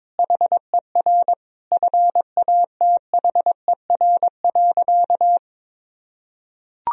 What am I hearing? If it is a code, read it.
Morse code, heard as a keyed tone, about 22 wpm, 700 Hz. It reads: HER FATHER.
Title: Sentences from Top 1000 Words - Review Part 4 - 22wpm